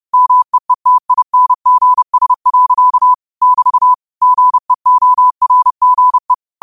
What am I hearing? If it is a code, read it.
Morse code, heard as a keyed tone, about 30 wpm, 1000 Hz. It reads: MEETINGS. <BT> GEORGE
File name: qotdfiles / 30wpm